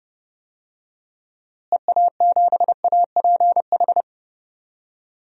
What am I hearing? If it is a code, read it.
Morse code, heard as a keyed tone, about 30 wpm, 700 Hz. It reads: EA7APH
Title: DXCC-100 Call Signs - 6 Characters - Repeated 3x - 30wpm